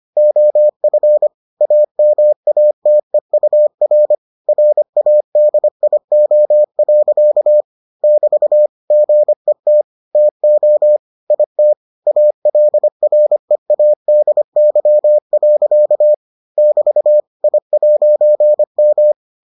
Morse code, heard as a keyed tone, about 25 wpm, 600 Hz. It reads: OF AMATEUR RADIO. <BT> GET TO IT ALREADY. <BT> I'M